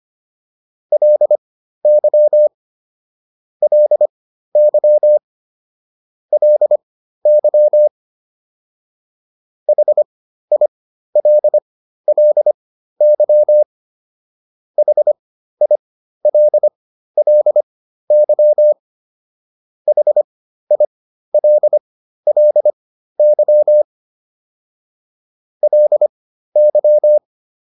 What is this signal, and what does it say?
Morse code, heard as a keyed tone, about 25 wpm, spaced as for 13 wpm, 600 Hz. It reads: LY LY LY HILLY HILLY HILLY LY
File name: LY5 13